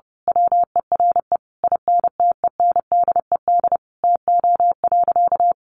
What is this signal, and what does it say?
Morse code, heard as a keyed tone, about 30 wpm, 700 Hz. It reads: WERE INTENDED TO.
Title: A Sound Of Thunder 30wpm